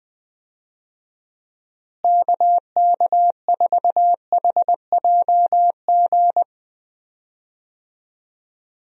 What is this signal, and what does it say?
Morse code, heard as a keyed tone, about 21 wpm, 700 Hz. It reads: KK4HJG
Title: USA Call Signs - 6 Characters - Repeated 3x - 20wpm